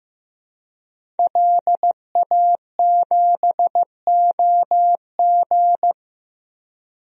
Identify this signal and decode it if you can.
Morse code, heard as a keyed tone, about 15 wpm, 700 Hz. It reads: LA7OG
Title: DXCC-100 Call Signs - 5 Characters - Repeated 3x - 15wpm